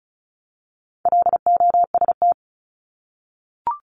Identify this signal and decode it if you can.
Morse code, heard as a keyed tone, about 34 wpm, 700 Hz. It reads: LOST
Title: Top 1251-1500 Words 35wpm